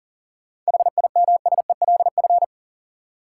Morse code, heard as a keyed tone, about 40 wpm, 700 Hz. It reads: HIMSELF
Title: Seven Letter Words - 40wpm